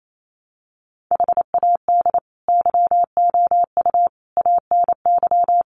Morse code, heard as a keyed tone, about 28 wpm, 700 Hz. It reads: HAD YOU ANY
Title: Sentences from Top 1500 Words 28wpm